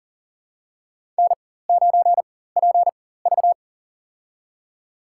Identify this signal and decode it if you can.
Morse code, heard as a keyed tone, about 40 wpm, spaced as for 19 wpm, 700 Hz. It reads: N9PV
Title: USA Call Signs - 4x Character Spacing - 40wpm